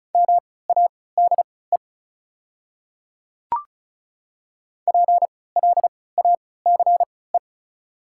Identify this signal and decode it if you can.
Morse code, heard as a keyed tone, about 36 wpm, spaced as for 20 wpm, 690 Hz. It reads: MADE PLACE
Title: 3x Character Spacing - Top 200 Words - 35wpm